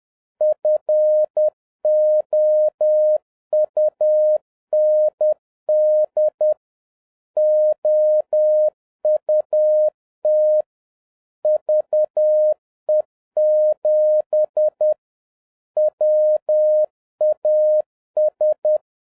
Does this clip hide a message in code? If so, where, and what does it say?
Yes: Morse code from 0.4 to 18.8 s, heard as a keyed tone, about 10 wpm, 610 Hz. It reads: FOUND OUT VE7 WAS